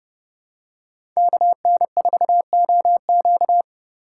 Morse code, heard as a keyed tone, about 30 wpm, 700 Hz. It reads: KN4OQ
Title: USA Call Signs - 5 Characters - 4x Character Spacing - 30wpm